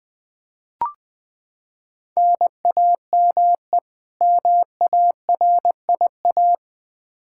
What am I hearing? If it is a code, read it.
Morse code, heard as a keyed tone, about 20 wpm, 700 Hz. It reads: NAME MARIA
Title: Practice QSO - Exchange 1 - Beginning - 20wpm